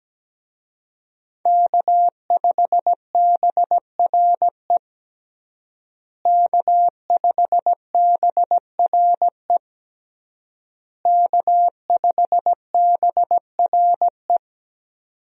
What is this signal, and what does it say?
Morse code, heard as a keyed tone, about 17 wpm, 700 Hz. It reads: K5BRE K5BRE K5BRE
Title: USA Call Signs - 5 Characters - Repeated 3x - 17wpm